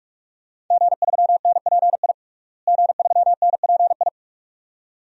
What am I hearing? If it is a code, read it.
Morse code, heard as a keyed tone, about 46 wpm, 700 Hz. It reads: G3NPI G3NPI
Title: DXCC-100 Call Signs - 5 Characters - Repeated 2x - 45wpm